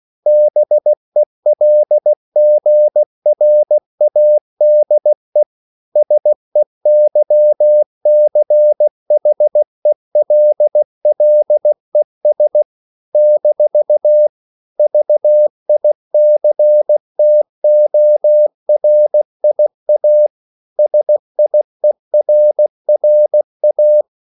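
Morse code, heard as a keyed tone, about 16 wpm, 600 Hz. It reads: BELGRADE SEYCHELLES - VICTORIA SIERRA
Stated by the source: CW practice of national capitals at 16 words per minute, from ON6ZQ.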